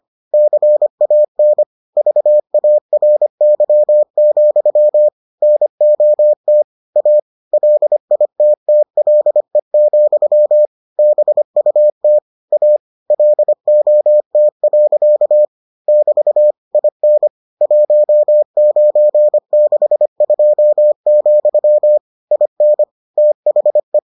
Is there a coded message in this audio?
Yes: Morse code from 0.3 to 24.0 s, heard as a keyed tone, about 25 wpm, 600 Hz. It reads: CAN VARY, NOT A LITTLE, BUT A LOT. <BT> IN 1962, IN THE